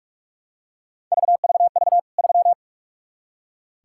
Morse code, heard as a keyed tone, about 46 wpm, 700 Hz. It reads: VVV 3
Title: K1USN SST QSO Practice 45wpm